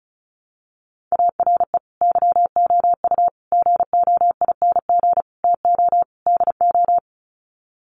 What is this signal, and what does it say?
Morse code, heard as a keyed tone, about 34 wpm, 700 Hz. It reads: ARE YOU GOING TO DO
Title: Sets of 5 Words 35wpm